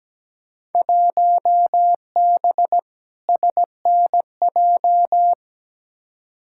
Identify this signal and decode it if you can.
Morse code, heard as a keyed tone, about 17 wpm, 700 Hz. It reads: 1B SNJ
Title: Field Day Exchanges - 17wpm